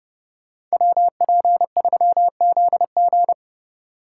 Morse code, heard as a keyed tone, about 34 wpm, 700 Hz. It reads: WP3ZG
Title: Speed Racing - USA Call Signs 30wpm